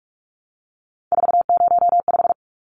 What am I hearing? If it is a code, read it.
Morse code, heard as a keyed tone, about 44 wpm, 700 Hz. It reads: 405
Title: Numbers - 3-Digit 45wpm